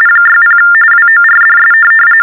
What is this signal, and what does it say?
TTY/TDD text telephone tones, about 45.45 baud. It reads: 60 WORDS
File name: teletype-RTTY-45